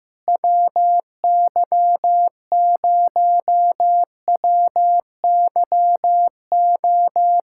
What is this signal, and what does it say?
Morse code, heard as a keyed tone, about 15 wpm, 700 Hz. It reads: WY0WYO
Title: SOTA QSO Practice 15wpm